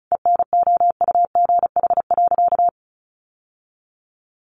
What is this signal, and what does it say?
Morse code, heard as a keyed tone, about 34 wpm, 700 Hz. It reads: ENOUGH.
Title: Sentences from Top 1200 Words 35wpm